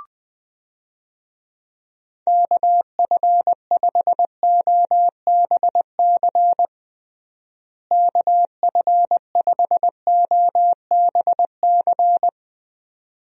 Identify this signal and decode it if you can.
Morse code, heard as a keyed tone, about 21 wpm, 700 Hz. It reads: KF5OBC KF5OBC
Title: USA Call Signs - 6 Characters - Repeated 2x - 20wpm